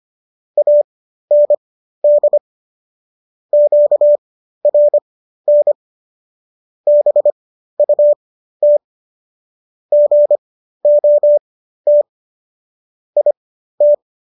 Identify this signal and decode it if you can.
Morse code, heard as a keyed tone, about 25 wpm, spaced as for 13 wpm, 590 Hz. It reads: AND QRN BUT GOT IT